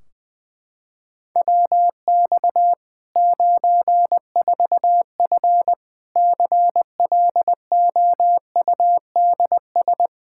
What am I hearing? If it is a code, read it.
Morse code, heard as a keyed tone, about 20 wpm, 700 Hz. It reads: WX 94F CLOUDS
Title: Practice QSO - Exchange 2 - Info - 20wpm